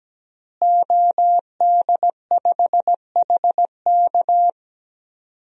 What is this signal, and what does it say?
Morse code, heard as a keyed tone, about 17 wpm, 700 Hz. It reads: OD5HK
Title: DXCC-100 Call Signs - 5 Characters - Repeated 3x - 17wpm